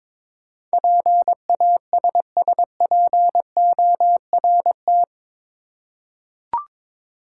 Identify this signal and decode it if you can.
Morse code, heard as a keyed tone, about 22 wpm, 700 Hz. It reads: PASSPORT
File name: practice / 22wpm / Eight Letter Words - 22wpm